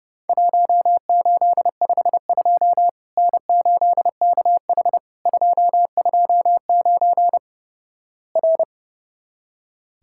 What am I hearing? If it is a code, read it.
Morse code, heard as a keyed tone, about 32 wpm, 700 Hz. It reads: 1852 N8KH 229 R
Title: Worked All Europe DX Contest - QTC Exchange - 30wpm